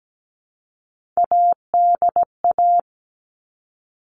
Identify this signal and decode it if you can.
Morse code, heard as a keyed tone, about 17 wpm, 700 Hz. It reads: ADA
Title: All Cut Numbers - 3-Digit 17wpm